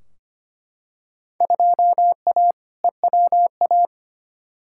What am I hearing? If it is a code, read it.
Morse code, heard as a keyed tone, about 25 wpm, 700 Hz. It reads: 2A EWA
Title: Field Day Exchanges - 25wpm